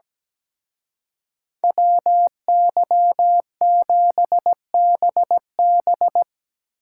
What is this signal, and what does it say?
Morse code, heard as a keyed tone, about 18 wpm, 700 Hz. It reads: WY7BB
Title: Speed Racing - USA 2x2 Call Signs 17wpm